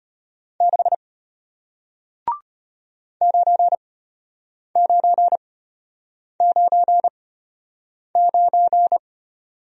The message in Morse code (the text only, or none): 6 9999